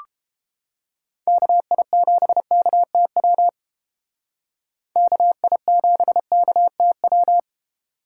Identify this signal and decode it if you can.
Morse code, heard as a keyed tone, about 31 wpm, 700 Hz. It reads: KI7KTW KI7KTW